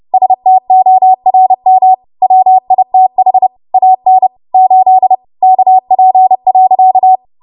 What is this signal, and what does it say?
Morse code, heard as a keyed tone, about 30 wpm, 750 Hz. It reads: STORM WITH AN 8 KP.
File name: Morse / 30WPM